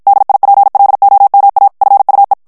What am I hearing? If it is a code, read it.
Morse code, heard as a keyed tone, about 55 wpm, 800 Hz. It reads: DIPLOMA FUE
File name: ESPAGNOL TEXTE EN CLAIR 1 VITESSE 50 MOTS MINUTES